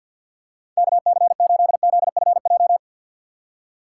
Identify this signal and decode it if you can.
Morse code, heard as a keyed tone, about 48 wpm, 700 Hz. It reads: KC8ZFJ
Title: USA Call Signs - 6 Characters - 4x Character Spacing - 50wpm